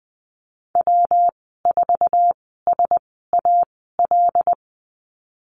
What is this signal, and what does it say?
Morse code, heard as a keyed tone, about 20 wpm, spaced as for 15 wpm, 700 Hz. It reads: W4SAL
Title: 161 Lesson 26g - Letter g - Call Signs 20wpm